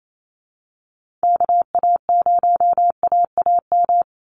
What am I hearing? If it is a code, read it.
Morse code, heard as a keyed tone, about 28 wpm, 700 Hz. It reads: KA0AAM
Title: Call Signs Repeated 2x - 28wpm